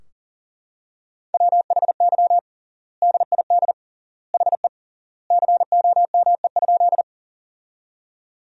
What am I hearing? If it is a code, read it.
Morse code, heard as a keyed tone, about 37 wpm, 700 Hz. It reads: WHY DID HE COME?
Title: 3x Word-Spacing - Sentences from Top 200 Words - 40wpm